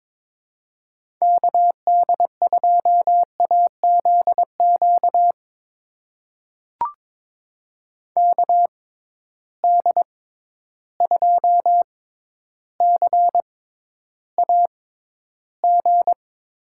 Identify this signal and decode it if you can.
Morse code, heard as a keyed tone, about 21 wpm, spaced as for 11 wpm, 700 Hz. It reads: KD2AZQ KD2CAG
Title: USA Call Signs - 6 Characters - 6x Character Spacing - 22wpm